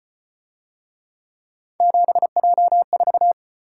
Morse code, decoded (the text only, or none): 7J4